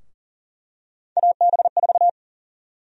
Morse code, heard as a keyed tone, about 40 wpm, 700 Hz. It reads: AB4
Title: USA Call Sign Prefixes - Repeated 2x - 40wpm